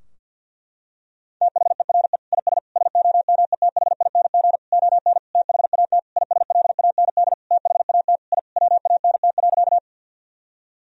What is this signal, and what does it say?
Morse code, heard as a keyed tone, about 52 wpm, 700 Hz. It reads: THERE IS SOMETHING ON THAT ISLAND THAT I WANT.
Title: Sentences from Top 1000 Words - Review Part 3 - 50wpm